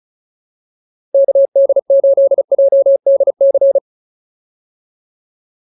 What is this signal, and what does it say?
Morse code, heard as a keyed tone, about 34 wpm, 550 Hz. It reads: KD8JDC